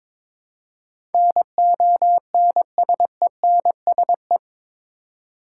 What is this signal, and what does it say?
Morse code, heard as a keyed tone, about 22 wpm, 700 Hz. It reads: NONSENSE
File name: practice / 22wpm / Eight Letter Words - 22wpm